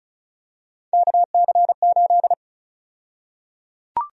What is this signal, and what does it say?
Morse code, heard as a keyed tone, about 34 wpm, 700 Hz. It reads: KC8
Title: USA Call Sign Prefixes - 35wpm